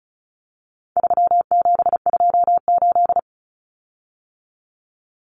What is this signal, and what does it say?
Morse code, heard as a keyed tone, about 34 wpm, 700 Hz. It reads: 3728
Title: Common Cut Numbers - 4-Digit 35wpm